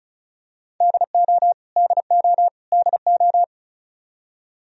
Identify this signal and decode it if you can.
Morse code, heard as a keyed tone, about 36 wpm, 700 Hz. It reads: DO DO DO
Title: Top 100 Words Repeated 35wpm